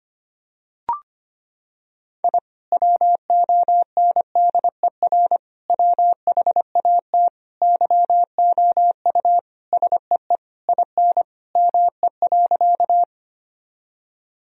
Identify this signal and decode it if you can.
Morse code, heard as a keyed tone, about 25 wpm, 700 Hz. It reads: I WONDER WHAT YOU SEE IN ME.
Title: Sentences from Top 1000 Words - Review Part 3 - 25wpm